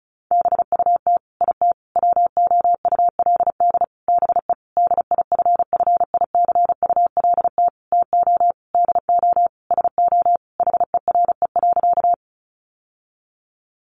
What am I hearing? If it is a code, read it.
Morse code, heard as a keyed tone, about 34 wpm, 700 Hz. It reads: BUT IT WOULD BE DIFFICULT TO DO SO HERE.
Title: Sentences from Top 600 Words 35wpm